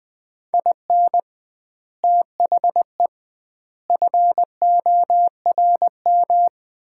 Morse code, heard as a keyed tone, about 20 wpm, 700 Hz. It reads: IN THE FORM
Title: 2x Word-Spacing - Sets of 5 Words - 20wpm